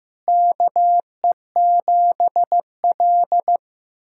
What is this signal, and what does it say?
Morse code, heard as a keyed tone, about 16 wpm, 700 Hz. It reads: KE7L
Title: SOTA QSO Practice 15wpm